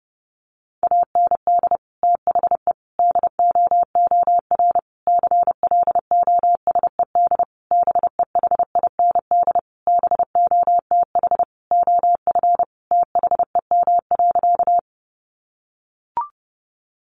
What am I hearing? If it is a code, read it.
Morse code, heard as a keyed tone, about 30 wpm, 700 Hz. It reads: AND THE DOOR CLOSED BEHIND BOTH OF THEM.
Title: Sentences from Top 1400 Words 30wpm